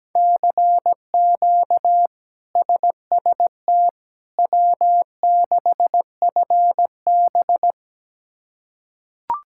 Morse code, heard as a keyed tone, about 17 wpm, 700 Hz. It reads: CQ SST W6FB
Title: K1USN SST QSO Practice - 17wpm